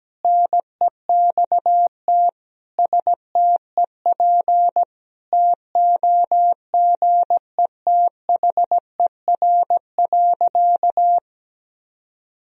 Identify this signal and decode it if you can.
Morse code, heard as a keyed tone, about 17 wpm, 700 Hz. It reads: NEXT STEP TOGETHER.
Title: Sentences from Top 1000 Words - Review Part 2 - 17wpm